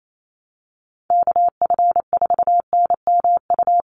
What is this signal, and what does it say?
Morse code, heard as a keyed tone, about 28 wpm, 700 Hz. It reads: KF4NMU
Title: USA Call Signs Repeated 3x 28wpm